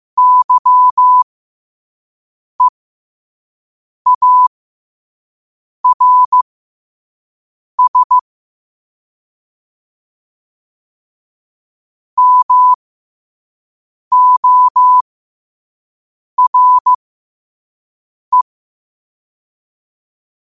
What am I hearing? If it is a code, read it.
Morse code, heard as a keyed tone, about 15 wpm, spaced as for 5 wpm, 1000 Hz. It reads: YEARS MORE
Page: New quotes every day in morse code at 5 Words per minute.